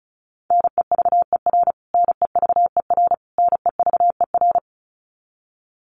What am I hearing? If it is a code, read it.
Morse code, heard as a keyed tone, about 34 wpm, 700 Hz. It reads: NEVER NEVER NEVER
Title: Top 300 Words Repeated 35wpm